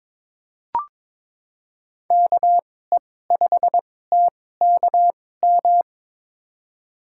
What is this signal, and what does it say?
Morse code, heard as a keyed tone, about 22 wpm, spaced as for 16 wpm, 700 Hz. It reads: KE5TKM
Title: USA Call Signs - 6 Characters - 2x Character Spacing - 22wpm